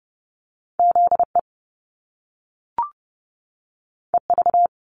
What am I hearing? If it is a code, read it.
Morse code, heard as a keyed tone, about 30 wpm, 700 Hz. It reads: ZE EV
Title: Top Two Letter Combinations in Words 30wpm